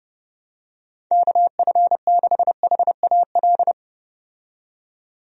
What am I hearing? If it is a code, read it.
Morse code, heard as a keyed tone, about 30 wpm, 700 Hz. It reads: KF6HAL
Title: USA Call Signs - 6 Characters - 30wpm